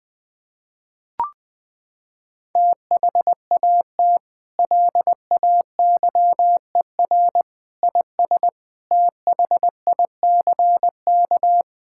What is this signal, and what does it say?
Morse code, heard as a keyed tone, about 20 wpm, 700 Hz. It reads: THAT LAYER IS THICK
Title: Sentences from Top 1600 Words 20wpm